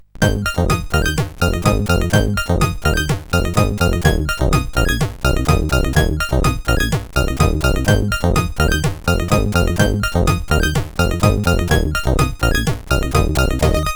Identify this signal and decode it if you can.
There is no code.